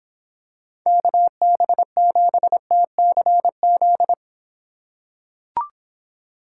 Morse code, heard as a keyed tone, about 26 wpm, 700 Hz. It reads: KB7TCZ